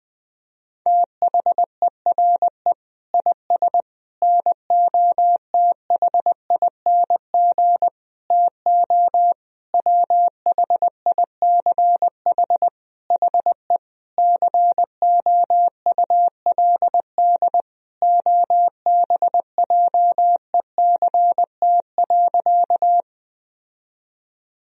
Morse code, heard as a keyed tone, about 20 wpm, 700 Hz. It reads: THERE IS NOTHING TO WHICH HE COULD OBJECT.
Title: Sentences from Top 1000 Words - Review Part 3 - 20wpm